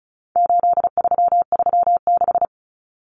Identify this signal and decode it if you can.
Morse code, heard as a keyed tone, about 34 wpm, 700 Hz. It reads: 8336
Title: Common Cut Numbers - 4-Digit 35wpm